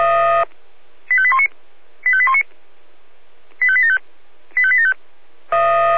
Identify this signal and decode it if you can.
Five-tone selective call (ZVEI), heard as quick runs of five tones, tones of about 70 ms. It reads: 85819 85819 85875 85875
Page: /SDR/ZVEI/